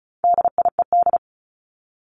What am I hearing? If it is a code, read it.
Morse code, heard as a keyed tone, about 34 wpm, 700 Hz. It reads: DIED
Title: ED Suffix - Top 100 Words with Base Words 35wpm